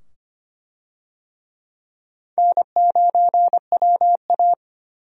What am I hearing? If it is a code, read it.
Morse code, heard as a keyed tone, about 25 wpm, 700 Hz. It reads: N9WA
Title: USA Call Signs - 4 Characters - 25wpm